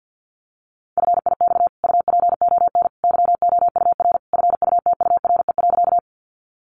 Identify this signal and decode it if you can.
Morse code, heard as a keyed tone, about 50 wpm, 700 Hz. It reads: FIX UPON YOUR FUTURE.
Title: Sentences from Top 1400 Words 50wpm